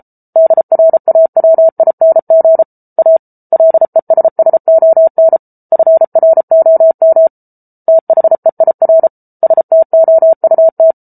Morse code, heard as a keyed tone, about 34 wpm, 650 Hz. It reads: DRAWING A LESSON FROM THEIR STOUT